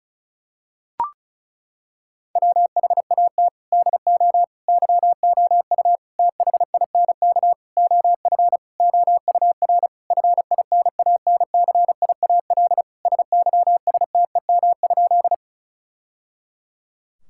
Morse code, heard as a keyed tone, about 36 wpm, 700 Hz. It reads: WHAT DO YOU THINK OF OUR FINANCIAL SYSTEM?
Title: Sentences from Top 1000 Words - Review Part 5 - 35wpm